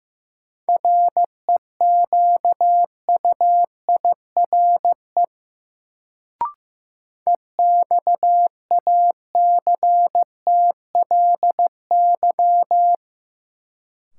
Morse code, heard as a keyed tone, about 15 wpm, 700 Hz. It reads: REQUIRE EXACTLY
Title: Sentences from Top 900 Words - New Word Drill - 15wpm